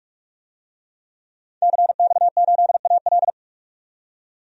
Morse code, heard as a keyed tone, about 43 wpm, 700 Hz. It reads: CX8AL
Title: DXCC-100 Call Signs - 5 Characters - 45wpm